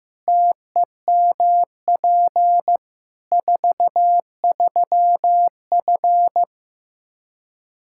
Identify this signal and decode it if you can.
Morse code, heard as a keyed tone, about 15 wpm, 700 Hz. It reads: TEMP 43F
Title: Practice QSO - Exchange 2 - Info - 15wpm